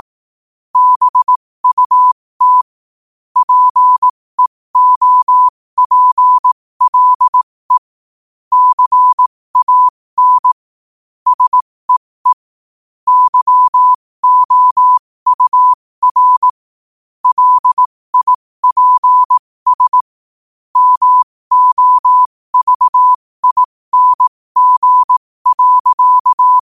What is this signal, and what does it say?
Morse code, heard as a keyed tone, about 18 wpm, spaced as for 15 wpm, 1000 Hz. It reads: BUT PEOPLE CAN SEE YOUR LIPS MOVING.
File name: qotdfiles / 15wpm